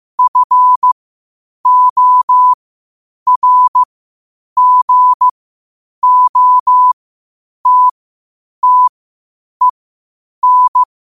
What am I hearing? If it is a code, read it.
Morse code, heard as a keyed tone, about 15 wpm, spaced as for 8 wpm, 1000 Hz. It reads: FORGOTTEN
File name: qotdfiles / 8wpm